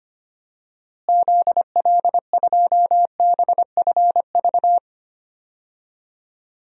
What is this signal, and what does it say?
Morse code, heard as a keyed tone, about 25 wpm, 700 Hz. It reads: ZL2BFV
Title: DXCC-100 Call Signs - 6 Characters - Repeated 3x - 25wpm